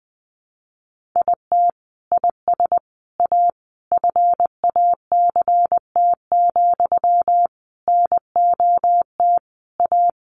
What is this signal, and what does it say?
Morse code, heard as a keyed tone, about 20 wpm, 700 Hz. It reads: IT IS A FACT, NOT A
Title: Sentences from Top 1400 Words 20wpm